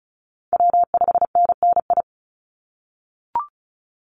Morse code, heard as a keyed tone, about 36 wpm, 700 Hz. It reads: W5NNI
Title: USA Call Signs 35wpm